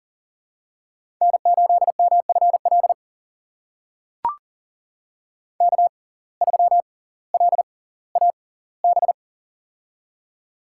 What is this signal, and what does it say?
Morse code, heard as a keyed tone, about 40 wpm, spaced as for 20 wpm, 700 Hz. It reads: N8MFL K3LAB
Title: USA Call Signs - 5 Characters - 6x Character Spacing - 40wpm